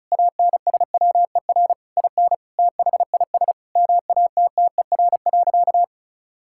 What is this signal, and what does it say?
Morse code, heard as a keyed tone, about 36 wpm, 700 Hz. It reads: ANSWER IN THIS MATTER.
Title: Sentences from Top 1600 Words 35wpm